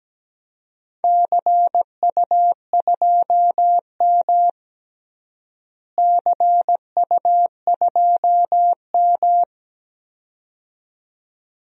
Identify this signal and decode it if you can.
Morse code, heard as a keyed tone, about 17 wpm, 700 Hz. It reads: CU2M CU2M
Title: DXCC-100 Call Signs - 4 Characters - Repeated 2x - 17wpm